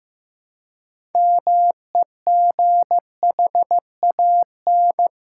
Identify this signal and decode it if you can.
Morse code, heard as a keyed tone, about 15 wpm, 700 Hz. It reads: MEGHAN
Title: Pangram Challenge v2 - 15wpm